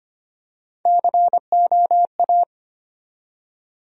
Morse code, heard as a keyed tone, about 25 wpm, 700 Hz. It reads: COA
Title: USA Call Sign Postfixes - 3 Letters 25wpm